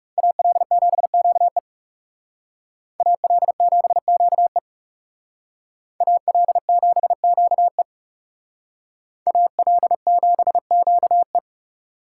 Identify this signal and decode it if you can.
Morse code, heard as a keyed tone, about 38 wpm, 700 Hz. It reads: AL7QE AL7QE AL7QE AL7QE